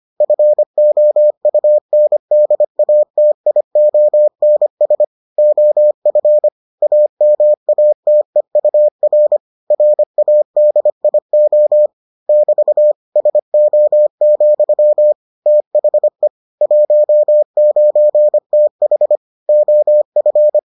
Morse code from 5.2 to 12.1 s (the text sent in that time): OF AMATEUR RADIO